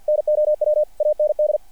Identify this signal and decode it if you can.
Morse code, heard as a keyed tone, about 50 wpm, 600 Hz. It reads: NOW AND
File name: archived / CRSnet